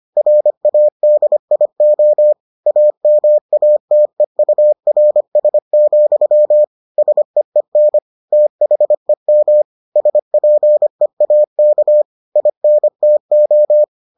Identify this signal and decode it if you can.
Morse code, heard as a keyed tone, about 25 wpm, 600 Hz. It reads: RADIO AMATEURS, SEEN THEM SPEAK INTO